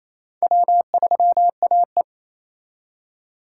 Morse code, heard as a keyed tone, about 33 wpm, 700 Hz. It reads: W3AE
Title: Speed Racing - USA 1x2 Call Signs 28wpm